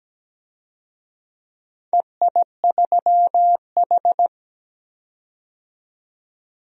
Morse code, heard as a keyed tone, about 17 wpm, 700 Hz. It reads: EI3H
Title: DXCC-100 Call Signs - 4 Characters - 17wpm